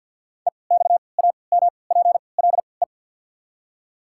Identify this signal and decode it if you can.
Morse code, heard as a keyed tone, about 53 wpm, spaced as for 29 wpm, 690 Hz. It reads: EXAMPLE
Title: 3x Character Spacing - Top 300 Words - 50wpm